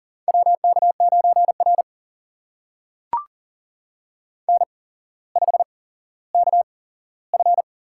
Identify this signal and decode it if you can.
Morse code, heard as a keyed tone, about 39 wpm, spaced as for 17 wpm, 700 Hz. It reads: WK9R N5KF
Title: USA Call Signs - 4 Characters - 8x Character Spacing - 40wpm